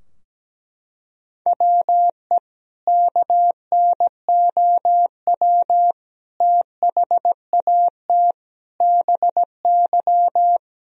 Morse code, heard as a keyed tone, about 17 wpm, 700 Hz. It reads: WE KNOW THAT BY
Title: Sentences from Top 1000 Words - Review Part 3 - 17wpm